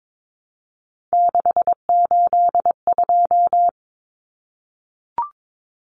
Morse code, heard as a keyed tone, about 22 wpm, 700 Hz. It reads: 682
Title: Common Cut Numbers - 3-Digit 22wpm